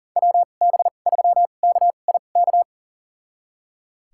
Morse code, heard as a keyed tone, about 37 wpm, spaced as for 28 wpm, 700 Hz. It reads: WB3KIK
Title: USA Call Signs - 6 Characters - 2x Character Spacing - 40wpm